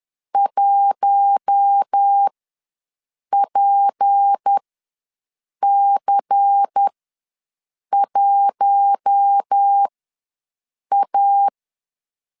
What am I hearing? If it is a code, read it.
Morse code, heard as a keyed tone, about 10 wpm, spaced as for 6 wpm, 800 Hz. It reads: 1PC1A
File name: LEZIONI / 6 PKX0 / 6 P ws 10wpm 3751